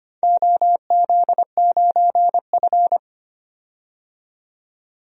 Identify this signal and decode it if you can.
Morse code, heard as a keyed tone, about 25 wpm, 700 Hz. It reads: OZ9F
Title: DXCC-100 Call Signs - 4 Characters - 25wpm